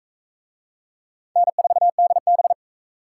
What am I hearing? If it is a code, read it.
Morse code, heard as a keyed tone, about 43 wpm, 700 Hz. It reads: N4DB